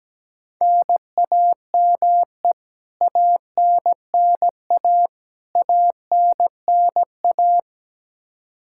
Morse code, heard as a keyed tone, about 17 wpm, 700 Hz. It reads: NAME ANNA ANNA
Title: Practice QSO - Exchange 1 - Beginning - 17wpm